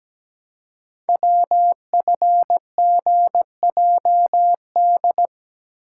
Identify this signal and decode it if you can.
Morse code, heard as a keyed tone, about 17 wpm, 700 Hz. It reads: WFGJD
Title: Code Groups - 5 Letter:Numbers - 17wpm